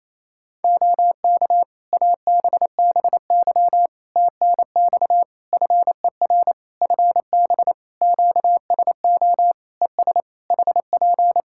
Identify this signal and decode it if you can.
Morse code, heard as a keyed tone, about 28 wpm, 700 Hz. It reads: OK ABBY TNX FER FB QSO ES HP
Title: Practice QSO - Exchange 4 - Ending - 28wpm